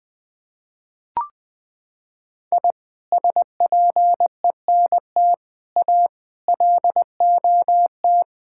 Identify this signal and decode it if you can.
Morse code, heard as a keyed tone, about 20 wpm, 700 Hz. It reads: I SPENT A LOT
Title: Sentences with 6-Word Phrases - 20wpm